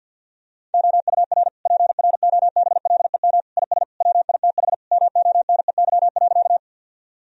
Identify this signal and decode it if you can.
Morse code, heard as a keyed tone, about 50 wpm, 700 Hz. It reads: OUR PROBLEM IS WITH MONEY.
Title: Sentences from Top 1000 Words - Review Part 2 - 50wpm